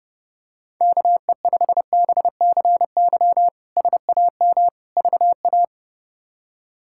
Morse code, heard as a keyed tone, about 32 wpm, 700 Hz. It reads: KE5BCY SAM VA
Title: K1USN SST QSO Practice - 30wpm